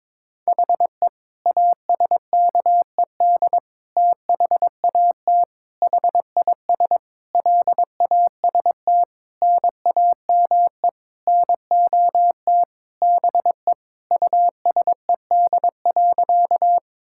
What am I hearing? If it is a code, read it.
Morse code, heard as a keyed tone, about 22 wpm, 700 Hz. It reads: HE ASKED THAT HIS LAST NAME NOT BE USED.
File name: Sentences with 9-Word Phrases - 22wpm